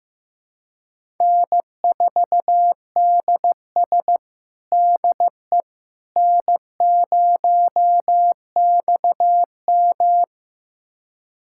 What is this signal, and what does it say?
Morse code, heard as a keyed tone, about 15 wpm, 700 Hz. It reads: N4DS DE N0XM
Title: USA Callsign DE Callsign - 4 Characters - 15wpm